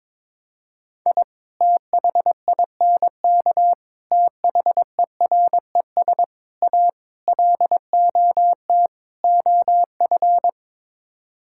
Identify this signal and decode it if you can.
Morse code, heard as a keyed tone, about 22 wpm, 700 Hz. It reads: I THINK THERES A LOT OF
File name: practice / 22wpm / Sets of 6 Words - 22wpm